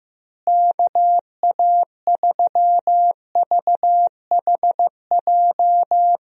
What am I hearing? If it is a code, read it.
Morse code, heard as a keyed tone, about 15 wpm, 700 Hz. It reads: KA3VHJ
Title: USA Call Signs - 6 Characters - Repeated 2x - 15wpm